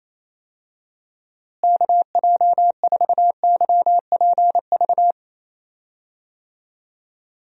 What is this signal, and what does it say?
Morse code, heard as a keyed tone, about 28 wpm, 700 Hz. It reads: KJ4YPV
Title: USA Call Signs - 6 Characters - Repeated 3x - 28wpm